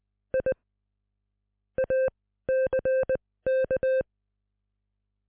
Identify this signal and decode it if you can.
Morse code, heard as a keyed tone, about 20 wpm, spaced as for 15 wpm, 520 Hz. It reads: I ACK